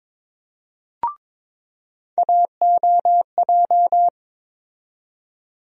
Morse code, heard as a keyed tone, about 22 wpm, 700 Hz. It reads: AOJ
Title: USA Call Sign Postfixes - 3 Letters 22wpm